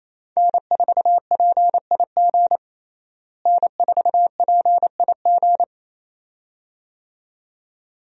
Morse code, heard as a keyed tone, about 28 wpm, 700 Hz. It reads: N4PIG N4PIG
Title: USA Call Signs - 5 Characters - Repeated 2x - 28wpm